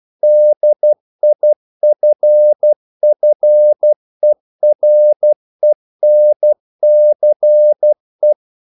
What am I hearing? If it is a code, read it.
Morse code, heard as a keyed tone, about 12 wpm, 600 Hz. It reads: DIFFERENCE
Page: English quotations in CW (telegraphy - morse)